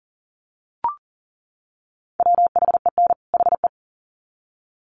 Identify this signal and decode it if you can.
Morse code, heard as a keyed tone, about 39 wpm, 700 Hz. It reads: WHEN HE
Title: Sets of 2 Words 40wpm